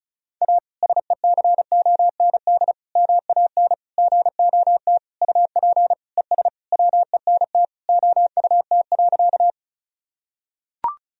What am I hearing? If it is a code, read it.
Morse code, heard as a keyed tone, about 34 wpm, 700 Hz. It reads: A SECOND MAN GOT UP ES WENT OUT.
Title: Sentences from Top 300 Words and Abbreviations 35wpm